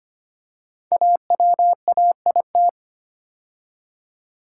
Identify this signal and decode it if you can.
Morse code, heard as a keyed tone, about 25 wpm, 700 Hz. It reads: AWAIT
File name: 5 Letter Words - First 3 Letters Given - 25wpm